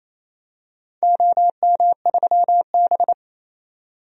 Morse code, heard as a keyed tone, about 28 wpm, 700 Hz. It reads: OM3B
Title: DXCC-100 Call Signs - 4 Characters - 28wpm